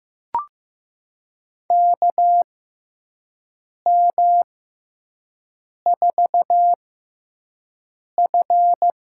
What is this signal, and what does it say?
Morse code, heard as a keyed tone, about 15 wpm, spaced as for 5 wpm, 700 Hz. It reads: KM4F
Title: USA Call Signs - 6x Character Spacing - 15wpm